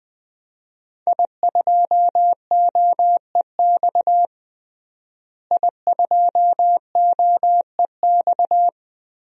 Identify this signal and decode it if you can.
Morse code, heard as a keyed tone, about 20 wpm, 700 Hz. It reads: I2OEX I2OEX
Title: DXCC-100 Call Signs - 5 Characters - Repeated 2x - 20wpm